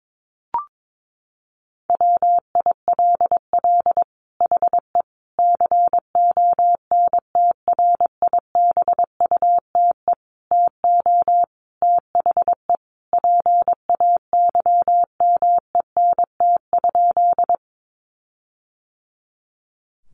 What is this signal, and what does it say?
Morse code, heard as a keyed tone, about 22 wpm, 700 Hz. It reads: WILL HE CONTRIBUTE TO THE PAYMENT?
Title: Sentences from Top 1400 Words 22wpm